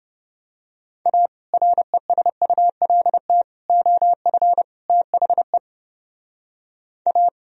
Morse code, heard as a keyed tone, about 30 wpm, 700 Hz. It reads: A RESULT OF THE A